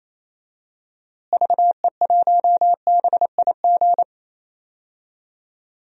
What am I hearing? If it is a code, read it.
Morse code, heard as a keyed tone, about 28 wpm, 700 Hz. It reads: VE1BIG
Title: DXCC-100 Call Signs - 6 Characters - Repeated 3x - 28wpm